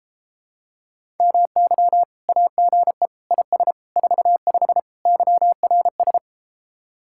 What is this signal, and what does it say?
Morse code, heard as a keyed tone, about 33 wpm, 700 Hz. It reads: MY AGE IS 45 YRS